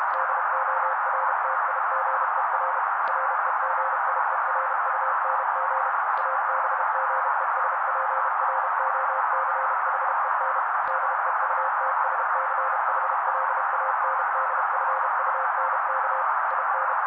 Morse code, heard as a keyed tone, about 32 wpm, 530 Hz. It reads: DORTIGER REGIERUNG ABGESPROCHEN DEUTSCHLANDFUNK IN
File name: daily / 31wpm noise